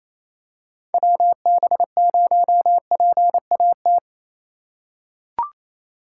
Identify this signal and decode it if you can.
Morse code, heard as a keyed tone, about 28 wpm, 700 Hz. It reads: WB0PAT
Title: USA Call Signs - 6 Characters - 2x Character Spacing - 28wpm